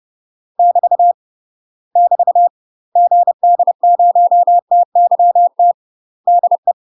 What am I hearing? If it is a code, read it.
Morse code, heard as a keyed tone, about 30 wpm, 700 Hz. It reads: <BT> <BT> GD0TYT DE
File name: UK MASTER QSOs TRAINING / 30WPM / UK MASTER QSOs 0007